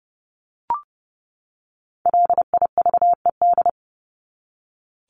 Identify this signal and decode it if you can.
Morse code, heard as a keyed tone, about 30 wpm, 700 Hz. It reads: LIVED
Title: ED Suffix - Top 100 Words with Base Words 30wpm